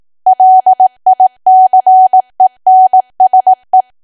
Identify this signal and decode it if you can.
Morse code, heard as a keyed tone, about 18 wpm, 750 Hz. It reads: LICENSE
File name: cwa / W1AW-Level2 / 18WPM